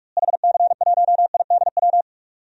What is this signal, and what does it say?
Morse code, heard as a keyed tone, about 46 wpm, 700 Hz. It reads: HC1IDW
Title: DXCC-100 Call Signs - 6 Characters - Repeated 3x - 45wpm